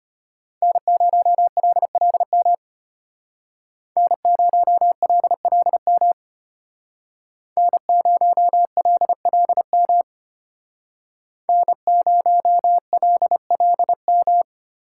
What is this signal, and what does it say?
Morse code, heard as a keyed tone, about 32 wpm, 700 Hz. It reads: N0LLM N0LLM N0LLM N0LLM